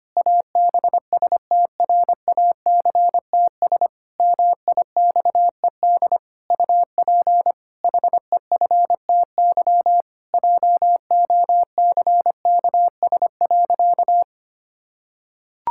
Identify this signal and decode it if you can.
Morse code, heard as a keyed tone, about 25 wpm, 700 Hz. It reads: ABSTRACTS MIXED UP HEFTY JOCKS.
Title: Pangram Challenge v2 - 25wpm